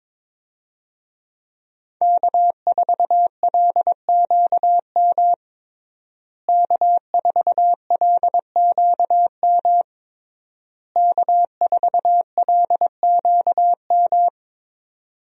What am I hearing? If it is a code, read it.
Morse code, heard as a keyed tone, about 22 wpm, 700 Hz. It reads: K4LQM K4LQM K4LQM
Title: USA Call Signs - 5 Characters - Repeated 3x - 22wpm